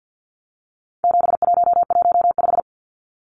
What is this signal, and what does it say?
Morse code, heard as a keyed tone, about 50 wpm, 700 Hz. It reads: 7115
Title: Numbers - 4-Digit 50wpm